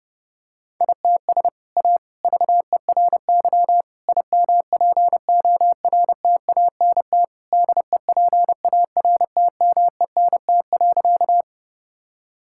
Morse code, heard as a keyed tone, about 30 wpm, 700 Hz. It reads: ITS A VERY IMPORTANT DEPARTMENT.
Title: Sentences from Top 1000 Words - Review Part 5 - 30wpm